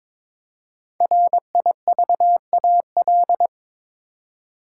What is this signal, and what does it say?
Morse code, heard as a keyed tone, about 22 wpm, 700 Hz. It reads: RIVAL
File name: 5 Letter Words - First 3 Letters Given - 22wpm